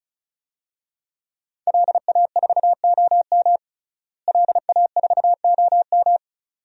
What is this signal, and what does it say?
Morse code, heard as a keyed tone, about 36 wpm, 700 Hz. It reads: LA4OM LA4OM
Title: DXCC-100 Call Signs - 5 Characters - Repeated 2x - 35wpm